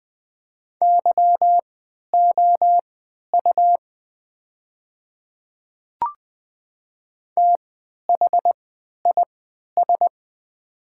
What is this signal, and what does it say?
Morse code, heard as a keyed tone, about 20 wpm, spaced as for 11 wpm, 700 Hz. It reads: YOU THIS
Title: 3x Character Spacing - Top 100 Words - 20wpm